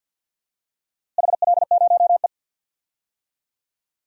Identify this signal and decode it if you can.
Morse code, heard as a keyed tone, about 45 wpm, 700 Hz. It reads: HL0E
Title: DXCC-100 Call Signs - 4 Characters - Repeated 2x - 50wpm